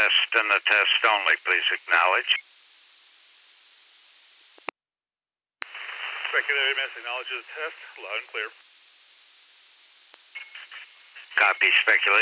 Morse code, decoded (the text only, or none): none